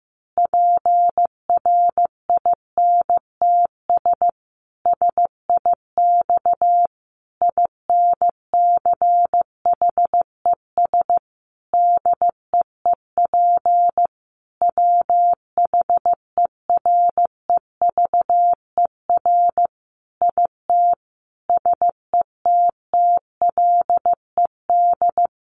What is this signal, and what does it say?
Morse code, heard as a keyed tone, about 15 wpm, 700 Hz. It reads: PRINTS SIX INCHES DEEP WHEREVER IT SETTLED
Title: A Sound Of Thunder 15wpm